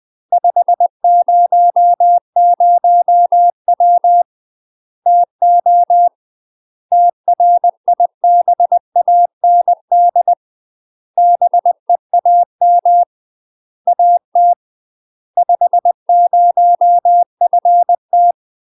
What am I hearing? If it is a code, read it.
Morse code, heard as a keyed tone, about 20 wpm, 700 Hz. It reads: 500W TO TRIBAND BEAM AT 50FT
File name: UK MASTER QSOs TRAINING / 20WPM / UK MASTER QSOs 0001